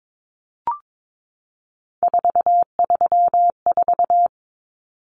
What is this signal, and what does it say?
Morse code, heard as a keyed tone, about 22 wpm, 700 Hz. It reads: 434
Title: Common Cut Numbers - 3-Digit 22wpm